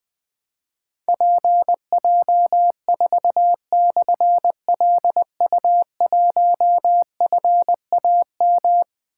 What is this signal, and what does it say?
Morse code, heard as a keyed tone, about 21 wpm, 700 Hz. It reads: PJ4/LU1FAM
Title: Difficult Call Signs 20wpm